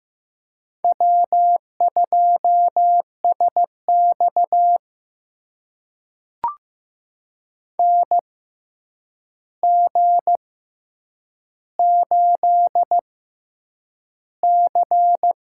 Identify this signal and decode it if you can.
Morse code, heard as a keyed tone, about 15 wpm, spaced as for 8 wpm, 700 Hz. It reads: W2SX NG8C
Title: USA Call Signs - 4 Characters - 6x Character Spacing - 15wpm